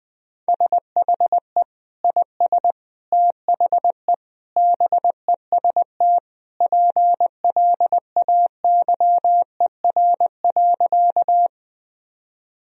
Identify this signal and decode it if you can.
Morse code, heard as a keyed tone, about 20 wpm, 700 Hz. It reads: SHE IS THE BEST PLAYER.
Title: Sentences from Top 1000 Words - Review Part 4 - 20wpm